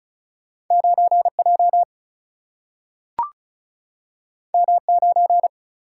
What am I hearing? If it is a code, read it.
Morse code, heard as a keyed tone, about 36 wpm, 700 Hz. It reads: 9 J M 9
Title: Single-Letter Single-Number 35wpm